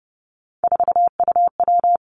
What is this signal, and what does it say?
Morse code, heard as a keyed tone, about 30 wpm, 700 Hz. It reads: 4UW
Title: Three Challenging Letter-Number Combinations 30wpm